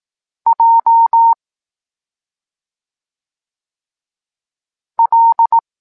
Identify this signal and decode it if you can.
Morse code, heard as a keyed tone, about 18 wpm, 920 Hz. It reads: J L
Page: Tekst 23.3 – Ciąg dalszy doskonalenia odbioru poznanych liter